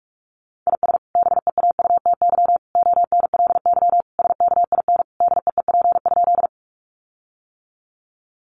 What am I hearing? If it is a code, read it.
Morse code, heard as a keyed tone, about 44 wpm, 700 Hz. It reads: IS BEAUTY ONLY SKIN DEEP?
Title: Sentences from Top 1300 Words 45wpm